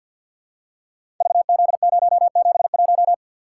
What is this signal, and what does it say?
Morse code, heard as a keyed tone, about 50 wpm, 700 Hz. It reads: VZ071
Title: Code Groups - 5 Letter:Numbers - 50wpm